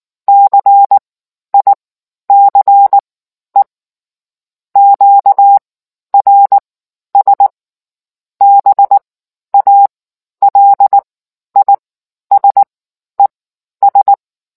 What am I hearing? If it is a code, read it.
Morse code, heard as a keyed tone, about 19 wpm, spaced as for 11 wpm, 800 Hz. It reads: CICE QRS BALISES